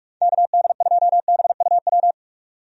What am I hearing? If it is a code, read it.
Morse code, heard as a keyed tone, about 44 wpm, 700 Hz. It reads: KD2BUW
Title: USA Call Signs - 6 Characters - 45wpm